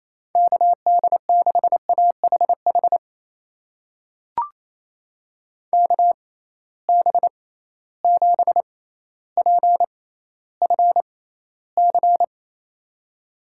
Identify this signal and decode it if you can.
Morse code, heard as a keyed tone, about 28 wpm, spaced as for 14 wpm, 700 Hz. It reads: KD6AHH KB7PFC
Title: USA Call Signs - 6 Characters - 6x Character Spacing - 28wpm